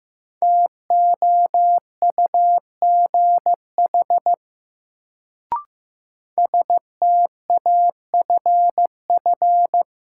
Morse code, heard as a keyed tone, about 15 wpm, 700 Hz. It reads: TOUGH STAFF
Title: Sentences from Top 1200 Words - New Word Drill - 15wpm